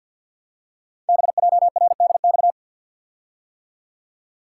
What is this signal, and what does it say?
Morse code, heard as a keyed tone, about 52 wpm, 700 Hz. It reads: BJRDX
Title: Code Groups - 5 Letters - 50wpm